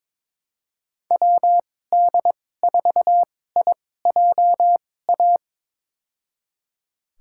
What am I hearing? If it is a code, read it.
Morse code, heard as a keyed tone, about 22 wpm, spaced as for 16 wpm, 700 Hz. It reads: WD4IJA
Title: USA Call Signs - 6 Characters - 2x Character Spacing - 22wpm